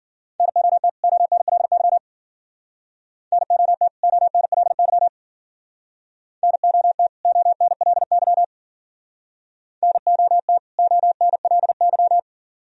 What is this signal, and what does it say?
Morse code, heard as a keyed tone, about 52 wpm, 700 Hz. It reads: NOT ONLY NOT ONLY NOT ONLY NOT ONLY